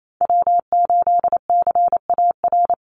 Morse code, heard as a keyed tone, about 28 wpm, 700 Hz. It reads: W8CAR
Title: Call Signs - 28wpm